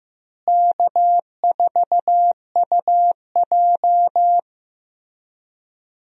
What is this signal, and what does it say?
Morse code, heard as a keyed tone, about 16 wpm, 700 Hz. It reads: K4UJ
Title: USA Call Signs - 4 Characters - Repeated 3x - 15wpm